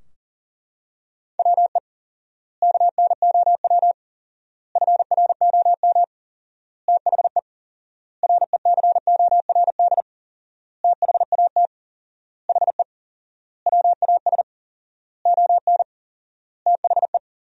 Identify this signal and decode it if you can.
Morse code, heard as a keyed tone, about 40 wpm, 700 Hz. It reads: WE KNOW FROM THE RECORD THAT HE WAS ON THE
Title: 4x Word-Spacing - Sentences from Top 500 Words - 40wpm